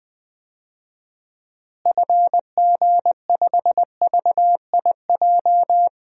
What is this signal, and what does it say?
Morse code, heard as a keyed tone, about 21 wpm, 700 Hz. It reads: FG5VIJ
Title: DXCC-100 Call Signs - 6 Characters - Repeated 2x - 20wpm